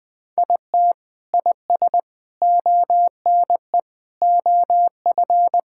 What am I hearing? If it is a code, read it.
Morse code, heard as a keyed tone, about 20 wpm, 700 Hz. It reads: IT IS ONE OF
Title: Sentences with 5-Word Phrases - Spoken Beginning - 20wpm